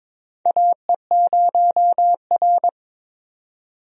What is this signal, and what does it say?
Morse code, heard as a keyed tone, about 22 wpm, 700 Hz. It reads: AE0R
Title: USA Call Signs - 4 Characters - 22wpm